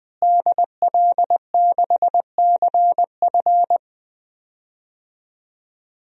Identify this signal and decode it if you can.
Morse code, heard as a keyed tone, about 20 wpm, 700 Hz. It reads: DL6CF
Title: DXCC-100 Call Signs - 5 Characters - Repeated 2x - 20wpm